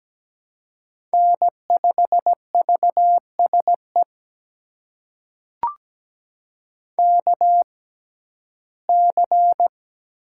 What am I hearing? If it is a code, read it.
Morse code, heard as a keyed tone, about 17 wpm, spaced as for 7 wpm, 700 Hz. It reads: N5VS EKC
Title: USA Call Signs - 5 Characters - 6x Character Spacing - 17wpm